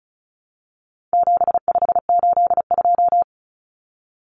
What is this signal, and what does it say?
Morse code, heard as a keyed tone, about 34 wpm, 700 Hz. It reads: 7582
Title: Numbers - 4-Digit 35wpm